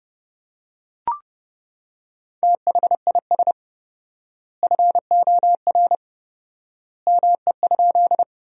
Morse code, heard as a keyed tone, about 30 wpm, 700 Hz. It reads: THIS FOR ME?
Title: 4x Word-Spacing - Sentences from Top 200 Words - 30wpm